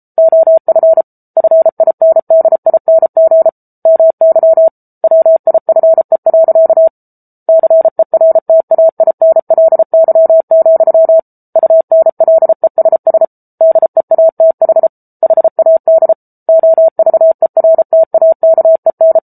Morse code, heard as a keyed tone, about 34 wpm, 650 Hz. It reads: OF FINDING MY WIFE. CERTAINLY, UNLESS DEATH HAD OVERTAKEN
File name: Book Two / 07-Chapter 7 - 33 WPM